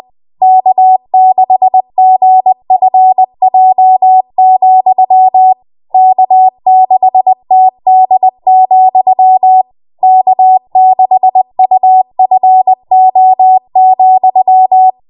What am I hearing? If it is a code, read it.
Morse code, heard as a keyed tone, about 20 wpm, 750 Hz. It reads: K6GFJ, K6TD, K6UFO,
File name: Morse / 20WPM